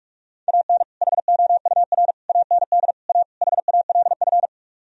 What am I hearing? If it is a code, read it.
Morse code, heard as a keyed tone, about 49 wpm, 700 Hz. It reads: AN HOUR AND A HALF